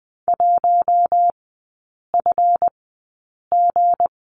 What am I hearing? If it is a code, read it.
Morse code, heard as a keyed tone, about 20 wpm, spaced as for 8 wpm, 690 Hz. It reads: 1FG
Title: 159 Lesson 26e - Letter g - Three Characters 20wpm